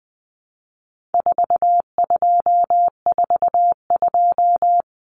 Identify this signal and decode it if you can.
Morse code, heard as a keyed tone, about 20 wpm, 700 Hz. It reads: 4242
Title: Numbers - 4-Digit 20wpm